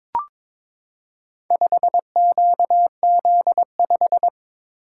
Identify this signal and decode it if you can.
Morse code, heard as a keyed tone, about 22 wpm, 700 Hz. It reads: 5QZ5
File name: practice / 22wpm / Four Letter-Number Combinations - 22wpm